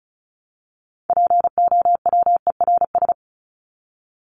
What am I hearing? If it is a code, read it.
Morse code, heard as a keyed tone, about 34 wpm, 700 Hz. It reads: POWERS
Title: ES:S Suffix - Top 100 Words with Base Words 35wpm